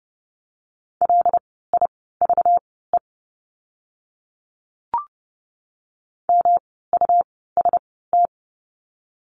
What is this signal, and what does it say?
Morse code, heard as a keyed tone, about 30 wpm, spaced as for 17 wpm, 700 Hz. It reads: LIVE MUST
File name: course / 30:10wpm / 237 Lesson 38f - Letter q - Words 30wpm